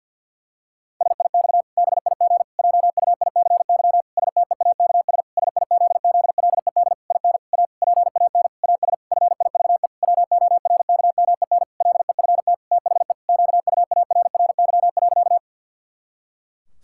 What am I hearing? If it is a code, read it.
Morse code, heard as a keyed tone, about 50 wpm, 700 Hz. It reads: SIX BIG JUICY STEAKS SIZZLED IN A PAN AS FIVE WORKMEN LEFT THE QUARRY.
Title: Pangram Challenge v2 - 50wpm